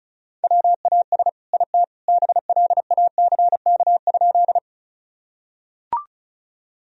Morse code, heard as a keyed tone, about 34 wpm, 700 Hz. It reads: WAS IT BLACK?
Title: Sentences from Top 1000 Words - Review Part 2 - 35wpm